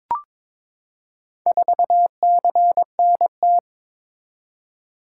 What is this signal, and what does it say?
Morse code, heard as a keyed tone, about 22 wpm, 700 Hz. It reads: 4CNT
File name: practice / 22wpm / Four Letter-Number Combinations - 22wpm